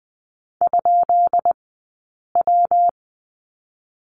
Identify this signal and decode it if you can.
Morse code, heard as a keyed tone, about 19 wpm, 700 Hz. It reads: ? W
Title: 132 Lesson 22d - Question Mark - Two Characters 20wpm